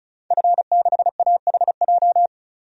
Morse code, heard as a keyed tone, about 34 wpm, 700 Hz. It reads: F6AHJ
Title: DXCC-100 Call Signs - 5 Characters - 35wpm